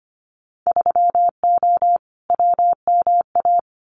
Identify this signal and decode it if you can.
Morse code, heard as a keyed tone, about 25 wpm, 700 Hz. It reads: 3O WMA
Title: Winter Field Day Exchanges Encore 25wpm